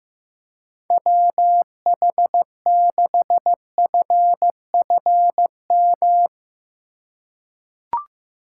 Morse code, heard as a keyed tone, about 16 wpm, 700 Hz. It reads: WH6FFM
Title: USA Call Signs - 8x Character Spacing - 15wpm